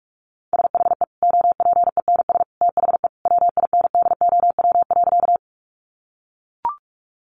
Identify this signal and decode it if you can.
Morse code, heard as a keyed tone, about 44 wpm, 700 Hz. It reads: SHE OPENS THE WINDOW.
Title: Additional Easy Phrases 45wpm